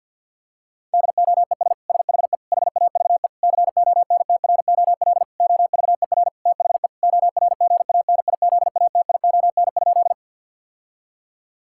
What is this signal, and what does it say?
Morse code, heard as a keyed tone, about 50 wpm, 700 Hz. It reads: DOES SHE HAVE CONTROL OVER THE ORGANIZATION?
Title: Sentences from Top 1000 Words - Review Part 5 - 50wpm